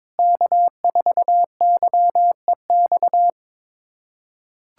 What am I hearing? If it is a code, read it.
Morse code, heard as a keyed tone, about 22 wpm, 700 Hz. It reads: K4YEX
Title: USA Callsign DE Callsign - 5 Characters - 22wpm